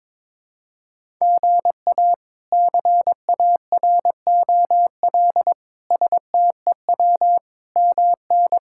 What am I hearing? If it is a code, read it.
Morse code, heard as a keyed tone, about 22 wpm, 700 Hz. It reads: GA CAROL STEW MN
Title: K1USN SST QSO Practice - 22wpm